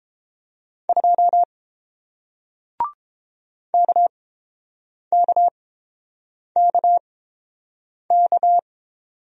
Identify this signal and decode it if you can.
Morse code, heard as a keyed tone, about 30 wpm, spaced as for 7 wpm, 700 Hz. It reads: 2 KKKK